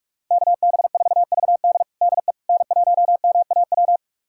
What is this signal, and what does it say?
Morse code, heard as a keyed tone, about 46 wpm, 700 Hz. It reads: KB4VD DE N1MAW
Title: USA Callsign DE Callsign - 5 Characters - 45wpm